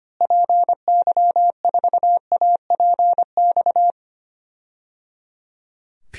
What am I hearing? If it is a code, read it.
Morse code, heard as a keyed tone, about 25 wpm, 700 Hz. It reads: PY4APX
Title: DXCC-100 Call Signs - 6 Characters - Repeated 3x - 25wpm